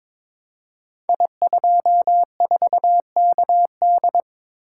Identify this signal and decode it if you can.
Morse code, heard as a keyed tone, about 22 wpm, 700 Hz. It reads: I24KD
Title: Code Groups - 5 Letter:Numbers - 22wpm